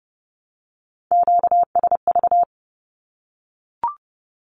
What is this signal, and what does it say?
Morse code, heard as a keyed tone, about 30 wpm, 700 Hz. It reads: QSV
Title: Q-Codes Repeated 3x 30wpm